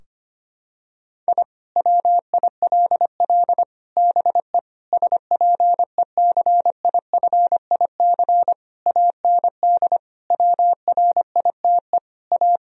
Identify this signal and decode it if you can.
Morse code, heard as a keyed tone, about 25 wpm, 700 Hz. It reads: I WILL BE SPECIFIC AND WRITE A
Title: Sentences from Top 1000 Words - Review Part 3 - 25wpm